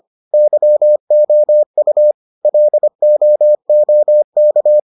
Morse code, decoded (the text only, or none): YOU LOOK